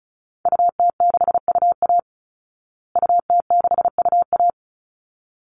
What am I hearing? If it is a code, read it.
Morse code, heard as a keyed tone, about 34 wpm, 700 Hz. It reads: UT6UA UT6UA
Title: Call Signs Repeated 3x - 35wpm